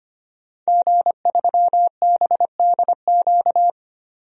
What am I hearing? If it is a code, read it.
Morse code, heard as a keyed tone, about 25 wpm, 700 Hz. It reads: G3BDQ
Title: DXCC-100 Call Signs - 5 Characters - 25wpm